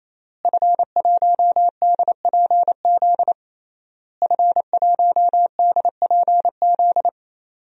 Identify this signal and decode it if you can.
Morse code, heard as a keyed tone, about 27 wpm, 700 Hz. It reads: F1DPZ F1DPZ
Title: DXCC-100 Call Signs - 5 Characters - Repeated 3x - 28wpm